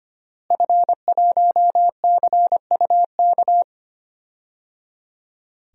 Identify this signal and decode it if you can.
Morse code, heard as a keyed tone, about 25 wpm, 700 Hz. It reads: F1CUK
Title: DXCC-100 Call Signs - 5 Characters - Repeated 3x - 25wpm